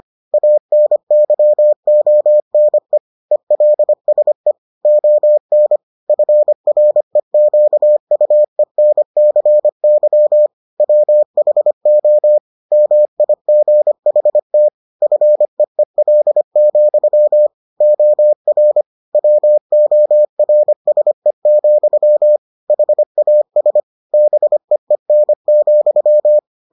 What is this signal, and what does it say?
Morse code, heard as a keyed tone, about 25 wpm, 600 Hz. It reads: ANYONE ELSE ON FREQUENCY WHO MIGHT FEEL, OR WORSE, HAS BEEN,